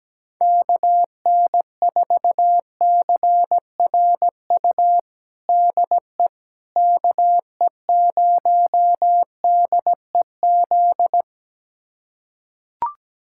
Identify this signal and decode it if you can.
Morse code, heard as a keyed tone, about 17 wpm, 700 Hz. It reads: KN4CRU DE KE0DEZ
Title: USA Callsign DE Callsign - 6 Characters - 17wpm